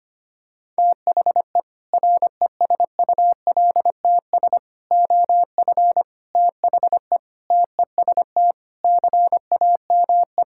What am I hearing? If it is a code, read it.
Morse code, heard as a keyed tone, about 25 wpm, 700 Hz. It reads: THE RESULTS OF THE TEST CAME
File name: practice / 25wpm / Sentences with 4-Word Phrases - 25wpm